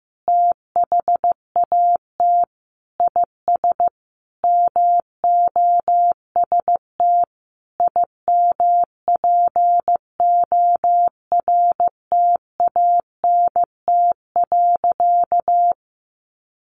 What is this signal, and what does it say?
Morse code, heard as a keyed tone, about 14 wpm, 700 Hz. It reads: THAT IS MOST IMPORTANT.
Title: Sentences from Top 1200 Words 15wpm